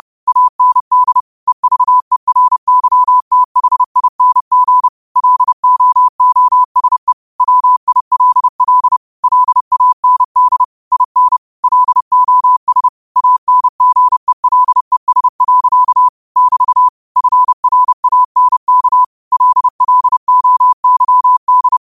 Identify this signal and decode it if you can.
Morse code, heard as a keyed tone, about 30 wpm, 1000 Hz. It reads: AND EVERYTHING LOOSE WILL LAND IN LOS ANGELES. <BT> FRANK LLOYD
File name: qotdfiles / 30wpm